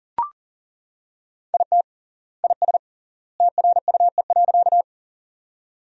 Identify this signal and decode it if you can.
Morse code, heard as a keyed tone, about 40 wpm, 700 Hz. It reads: IT IS TRUE.
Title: 3x Word-Spacing - Sentences from Top 400 Words - 40wpm